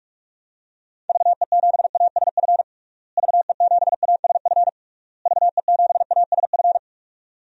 Morse code, heard as a keyed tone, about 41 wpm, 700 Hz. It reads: VE7ASF VE7ASF VE7ASF
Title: DXCC-100 Call Signs - 6 Characters - Repeated 3x - 45wpm